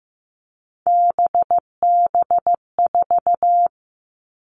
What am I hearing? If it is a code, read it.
Morse code, heard as a keyed tone, about 15 wpm, 700 Hz. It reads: BB4
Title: All Cut Numbers - 3-Digit 15wpm